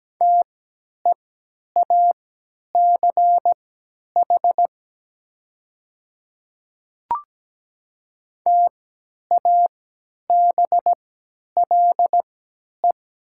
Morse code, heard as a keyed tone, about 17 wpm, spaced as for 10 wpm, 700 Hz. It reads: TEACH TABLE
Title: 3x Character Spacing - Top 500 Words - 17wpm